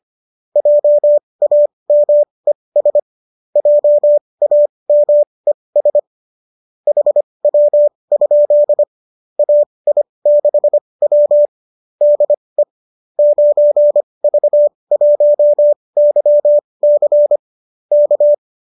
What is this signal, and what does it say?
Morse code, heard as a keyed tone, about 25 wpm, spaced as for 20 wpm, 590 Hz. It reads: JAMES JAMES HW? AI6W DE 9V1YC K
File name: qso207 20